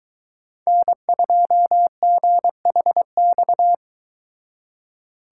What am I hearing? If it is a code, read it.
Morse code, heard as a keyed tone, about 23 wpm, 700 Hz. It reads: N2GHX